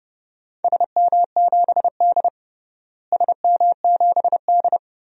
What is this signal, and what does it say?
Morse code, heard as a keyed tone, about 32 wpm, 700 Hz. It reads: SM7D SM7D
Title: DXCC-100 Call Signs - 4 Characters - Repeated 2x - 30wpm